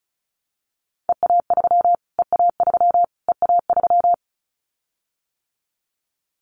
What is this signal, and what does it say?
Morse code, heard as a keyed tone, about 34 wpm, 700 Hz. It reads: EA3 EA3 EA3
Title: Callsign Prefixes for Top 25 DXCC - Repeated 3x - 35wpm